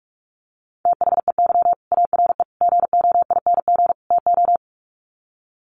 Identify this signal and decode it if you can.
Morse code, heard as a keyed tone, about 44 wpm, 700 Hz. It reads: THEY ARE GOING TO
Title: Sets of 4 Words 45wpm